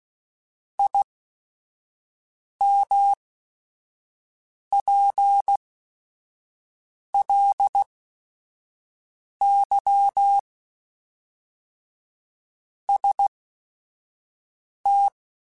Morse code, heard as a keyed tone, about 16 wpm, spaced as for 5 wpm, 800 Hz. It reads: IMPLY ST
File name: Technical Subjects / Technical Subjects-mp3s / XC @5